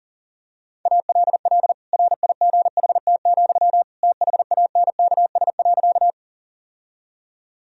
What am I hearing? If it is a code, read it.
Morse code, heard as a keyed tone, about 40 wpm, 700 Hz. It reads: ALL RIGHT, THANKS.
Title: Sentences from Top 1000 Words - Review Part 3 - 40wpm